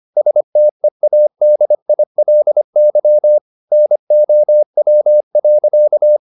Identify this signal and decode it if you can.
Morse code, heard as a keyed tone, about 25 wpm, 600 Hz. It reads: STEADILY NOW.